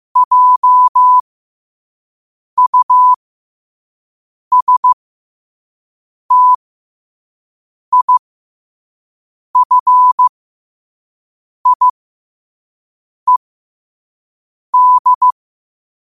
Morse code, heard as a keyed tone, about 15 wpm, spaced as for 5 wpm, 1000 Hz. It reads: JUSTIFIED
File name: 5wpm